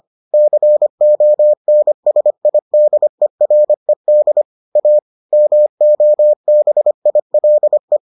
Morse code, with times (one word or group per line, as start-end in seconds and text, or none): 0.3-4.4 CONSIDERED
4.8-5.0 A
5.3-8.0 MOBILE